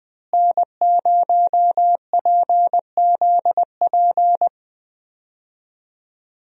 Morse code, heard as a keyed tone, about 21 wpm, 700 Hz. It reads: N0PZP
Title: USA Call Signs - 5 Characters - Repeated 3x - 20wpm